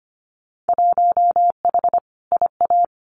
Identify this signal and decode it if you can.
Morse code, heard as a keyed tone, about 25 wpm, 700 Hz. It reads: 1H IA
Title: Winter Field Day Exchanges 25wpm